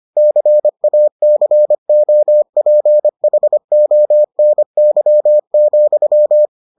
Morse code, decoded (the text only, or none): CACOPHONY,